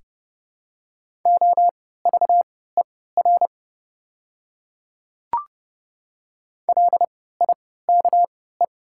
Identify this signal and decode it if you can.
Morse code, heard as a keyed tone, about 30 wpm, spaced as for 17 wpm, 700 Hz. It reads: OVER LIKE
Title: 3x Character Spacing - Top 100 Words - 30wpm